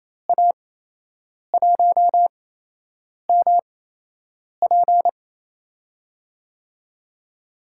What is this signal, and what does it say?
Morse code, heard as a keyed tone, about 27 wpm, spaced as for 8 wpm, 700 Hz. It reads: A1MP
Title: USA Call Signs - 8x Character Spacing - 28wpm